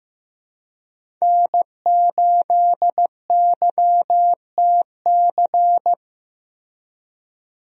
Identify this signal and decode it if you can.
Morse code, heard as a keyed tone, about 15 wpm, 700 Hz. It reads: N8YTC
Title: USA Call Signs - 5 Characters - Repeated 2x - 15wpm